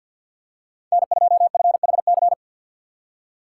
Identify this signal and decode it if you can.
Morse code, heard as a keyed tone, about 51 wpm, 700 Hz. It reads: N2FHC
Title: USA Call Signs - 5 Characters - 3x Character Spacing - 50wpm